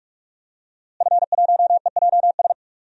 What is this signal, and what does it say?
Morse code, heard as a keyed tone, about 42 wpm, 700 Hz. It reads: F1EJS
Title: DXCC-100 Call Signs - 5 Characters - Repeated 3x - 45wpm